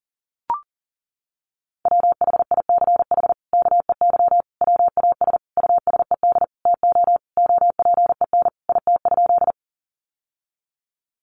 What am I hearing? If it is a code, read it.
Morse code, heard as a keyed tone, about 40 wpm, 700 Hz. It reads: WHICH KEY WAS USED TO OPEN IT?
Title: Sentences from Top 600 Words 40wpm